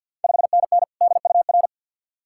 Morse code, heard as a keyed tone, about 51 wpm, 690 Hz. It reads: 5NN DUR
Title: Ontario QSO Party Exchange 50wpm